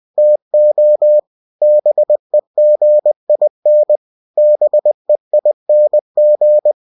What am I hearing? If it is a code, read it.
Morse code, heard as a keyed tone, about 20 wpm, 600 Hz. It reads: TO BEGIN BEING
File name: cw-books / winnie-the-pooh / winnie-20wpm-page-0063